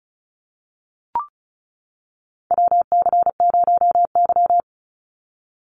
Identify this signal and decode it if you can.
Morse code, heard as a keyed tone, about 36 wpm, 700 Hz. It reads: WC0Y
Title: NA SOTA Chaser Callsigns 35wpm